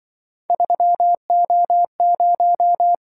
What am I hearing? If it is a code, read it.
Morse code, heard as a keyed tone, about 24 wpm, 700 Hz. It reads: 3O0
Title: Speed-Racing - Three Letter-Number Combinations - 20wpm